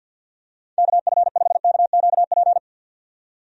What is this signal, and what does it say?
Morse code, heard as a keyed tone, about 51 wpm, 700 Hz. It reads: KF5KQP
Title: USA Call Signs - 6 Characters - 50wpm